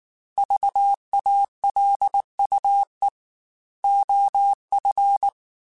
Morse code, heard as a keyed tone, about 19 wpm, 800 Hz. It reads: VALUE OF